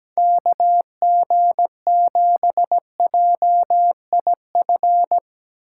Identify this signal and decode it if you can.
Morse code, heard as a keyed tone, about 17 wpm, 700 Hz. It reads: KG7JIF
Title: Speed Racing - USA Call Signs 17wpm